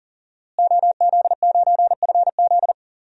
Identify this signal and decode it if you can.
Morse code, heard as a keyed tone, about 40 wpm, 700 Hz. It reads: OZ9FZ
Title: DXCC-100 Call Signs - 5 Characters - Repeated 3x - 40wpm